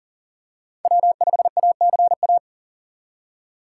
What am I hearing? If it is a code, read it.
Morse code, heard as a keyed tone, about 40 wpm, 700 Hz. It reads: W5ACA
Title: USA Call Signs - 5 Characters - 2x Character Spacing - 40wpm